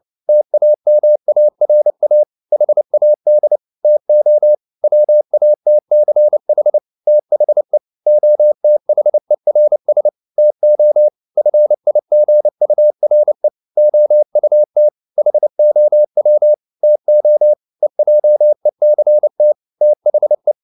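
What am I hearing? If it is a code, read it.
Morse code, heard as a keyed tone, about 29 wpm, 600 Hz. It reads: TAMARA HAD TO WATCH THE OTHERS TO FIGURE OUT HOW TO EJECT THE
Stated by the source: Morse code at 29 wpm, sending Ken Follett's novel Never in English, from CW med Gnister.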